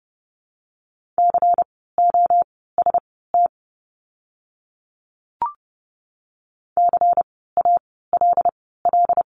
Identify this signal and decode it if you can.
Morse code, heard as a keyed tone, about 30 wpm, spaced as for 17 wpm, 700 Hz. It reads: COST CALL
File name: Lesson 16f - Letter c - Words 30wpm